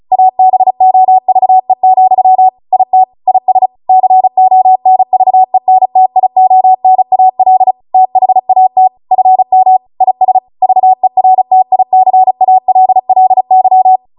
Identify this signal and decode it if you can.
Morse code, heard as a keyed tone, about 36 wpm, 750 Hz. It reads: ABOVE, IT IS CONVENTIONAL THAT FM IS VERTICALLY